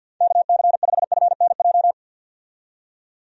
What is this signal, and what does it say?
Morse code, heard as a keyed tone, about 46 wpm, 700 Hz. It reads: KC5FNJ
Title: USA Call Signs - 6 Characters - 50wpm